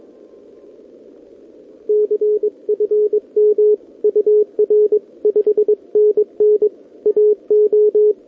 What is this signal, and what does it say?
Morse code, heard as a keyed tone, about 22 wpm, 440 Hz. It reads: CFM UR 5NN AO
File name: OK QRP 2026 / OKQRP2026